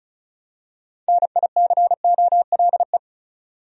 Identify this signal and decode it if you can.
Morse code, heard as a keyed tone, about 36 wpm, 700 Hz. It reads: NICOLE
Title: USA Common Names - 35wpm